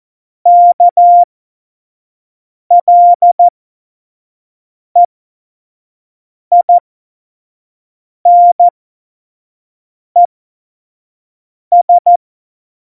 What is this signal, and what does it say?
Morse code, heard as a keyed tone, about 14 wpm, spaced as for 5 wpm, 700 Hz. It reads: KLEINES
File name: Morsekurs / Witze / Witz 24 April 2018, 25 F70